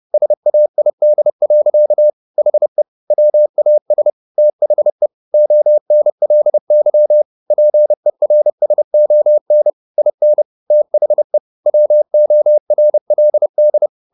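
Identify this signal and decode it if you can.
Morse code, heard as a keyed tone, about 30 wpm, 600 Hz. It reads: SAID. HE WAS THE ONLY PERSON IN THE WORLD
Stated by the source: Morse code at 30 wpm, the novel Never by Ken Follett in English, from CW med Gnister.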